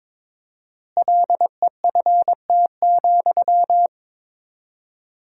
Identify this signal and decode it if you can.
Morse code, heard as a keyed tone, about 22 wpm, 700 Hz. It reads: LEFT,
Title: 4x Word-Spacing - Sentences from Top 400 Words - 22wpm